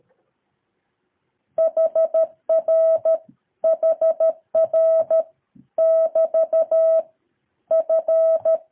Morse code, heard as a keyed tone, about 13 wpm, 650 Hz. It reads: HR HR <BT> F
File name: jh8cbh / 2023cw